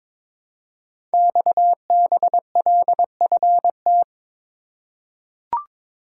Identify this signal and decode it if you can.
Morse code, heard as a keyed tone, about 22 wpm, 700 Hz. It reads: XBLFT
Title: Code Groups - 5 Letters - 22wpm